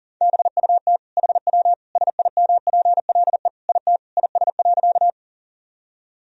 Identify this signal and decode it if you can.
Morse code, heard as a keyed tone, about 40 wpm, 700 Hz. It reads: BUT HW SIMPLE IT IS.
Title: Sentences from Top 400 Words and Abbreviations 40wpm